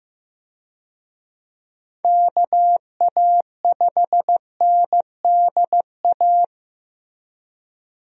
Morse code, heard as a keyed tone, about 15 wpm, 700 Hz. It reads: KA5NDA
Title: USA Call Signs - 6 Characters - Repeated 3x - 15wpm